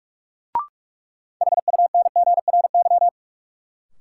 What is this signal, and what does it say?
Morse code, heard as a keyed tone, about 46 wpm, 700 Hz. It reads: HUNGRY
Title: Sentences from Top 1400 Words - New Word Drill - 45wpm